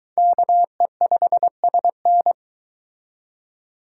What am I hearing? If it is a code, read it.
Morse code, heard as a keyed tone, about 23 wpm, 700 Hz. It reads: KE5SN